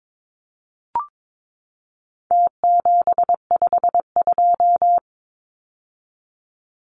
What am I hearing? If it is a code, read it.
Morse code, heard as a keyed tone, about 22 wpm, 700 Hz. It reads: T752
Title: Numbers - Contest Style - 4-Digit 22wpm